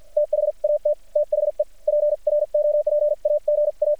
Morse code, heard as a keyed tone, about 48 wpm, spaced as for 41 wpm, 600 Hz. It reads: THAT THE PROPAGA
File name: archived / CRSnet